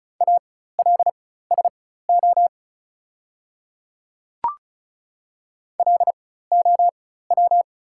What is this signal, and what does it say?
Morse code, heard as a keyed tone, about 36 wpm, spaced as for 16 wpm, 690 Hz. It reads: ALSO LOW
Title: 4x Character Spacing - Top 300 Words - 35wpm